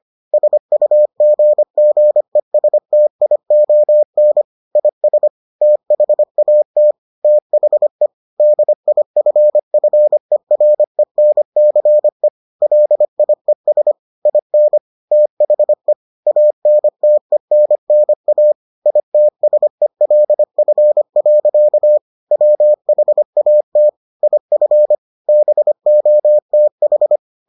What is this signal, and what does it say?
Morse code, heard as a keyed tone, about 25 wpm, 600 Hz. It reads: SUGGESTION IS THAT THE DIFFERENCE LIES IN THE ANTENNA ITSELF. WHAT IF BOTH